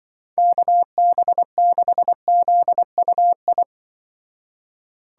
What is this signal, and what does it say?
Morse code, heard as a keyed tone, about 24 wpm, 700 Hz. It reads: KB6ZUI
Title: Speed Racing - USA Call Signs 20wpm